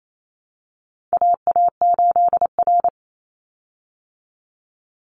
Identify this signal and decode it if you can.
Morse code, heard as a keyed tone, about 28 wpm, 700 Hz. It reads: AA8R
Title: Call Signs Repeated 3x - 28wpm